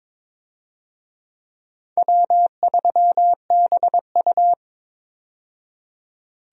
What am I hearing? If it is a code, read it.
Morse code, heard as a keyed tone, about 22 wpm, 700 Hz. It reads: W3BU
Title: USA Call Signs - 4 Characters - 22wpm